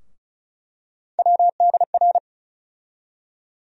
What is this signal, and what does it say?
Morse code, heard as a keyed tone, about 36 wpm, 700 Hz. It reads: WDR
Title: USA Call Sign Postfixes - 3 Letters 35wpm